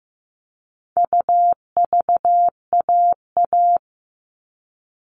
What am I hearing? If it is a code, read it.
Morse code, heard as a keyed tone, about 15 wpm, 700 Hz. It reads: UVAA
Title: All Cut Numbers - 4-Digit 15wpm